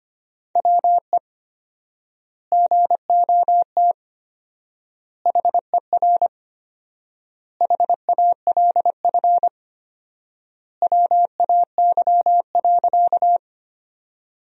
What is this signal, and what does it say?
Morse code, heard as a keyed tone, about 25 wpm, 700 Hz. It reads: WE GOT HER HALF WAY.
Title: 4x Word-Spacing - Sentences from Top 400 Words - 25wpm